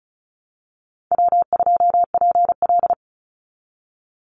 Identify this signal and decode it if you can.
Morse code, heard as a keyed tone, about 34 wpm, 700 Hz. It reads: W2PL
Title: Easy Call Signs Repeated 2x - 35wpm